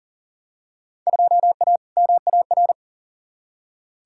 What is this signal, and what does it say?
Morse code, heard as a keyed tone, about 40 wpm, 700 Hz. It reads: 2A MAR
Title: Field Day Exchanges - 40wpm